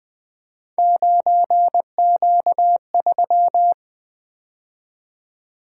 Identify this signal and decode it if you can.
Morse code, heard as a keyed tone, about 20 wpm, 700 Hz. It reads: 9Q3
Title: Three Letter-Number Combinations - 20wpm